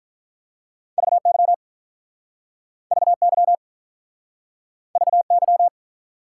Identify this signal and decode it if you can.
Morse code, heard as a keyed tone, about 48 wpm, 700 Hz. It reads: VY VY VY